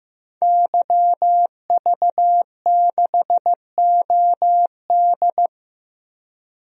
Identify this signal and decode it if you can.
Morse code, heard as a keyed tone, about 15 wpm, 700 Hz. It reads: YV6OD
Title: DXCC-100 Call Signs - 5 Characters - 15wpm